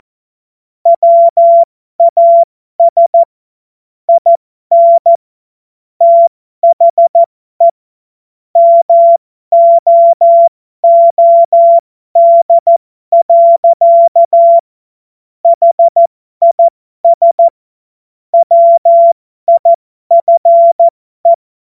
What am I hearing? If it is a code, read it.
Morse code, heard as a keyed tone, about 14 wpm, spaced as for 12 wpm, 680 Hz. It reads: WAS IN THE MOOD. HIS WIFE
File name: Morsekurs / Witze